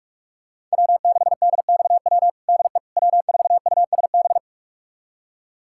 Morse code, heard as a keyed tone, about 41 wpm, 700 Hz. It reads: W6DXW DE W4USB
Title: USA Callsign DE Callsign - 5 Characters - 45wpm